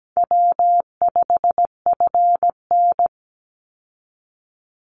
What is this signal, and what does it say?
Morse code, heard as a keyed tone, about 17 wpm, 700 Hz. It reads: W5FN
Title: Easy Call Signs Repeated 3x - 17wpm